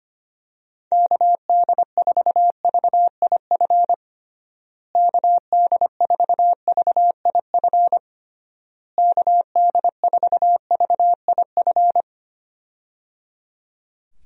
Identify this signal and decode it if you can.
Morse code, heard as a keyed tone, about 25 wpm, 700 Hz. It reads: KD4VIF KD4VIF KD4VIF
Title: USA Call Signs - 6 Characters - Repeated 3x - 25wpm